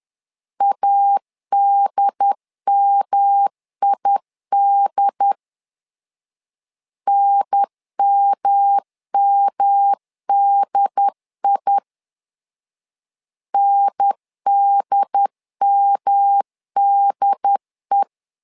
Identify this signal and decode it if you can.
Morse code, heard as a keyed tone, about 11 wpm, 800 Hz. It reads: ADMID NMMDI NDMDE
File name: LEZIONI / 3 MWRO / 3 M 10wpm 1828